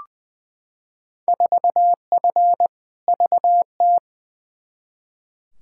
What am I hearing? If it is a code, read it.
Morse code, heard as a keyed tone, about 20 wpm, 700 Hz. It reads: 4F VT
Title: Field Day Exchanges Encore - 20wpm